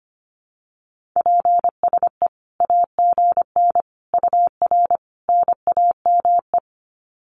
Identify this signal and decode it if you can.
Morse code, heard as a keyed tone, about 25 wpm, 700 Hz. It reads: PSE AGN UR NAME
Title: Common QSO Phrases 25wpm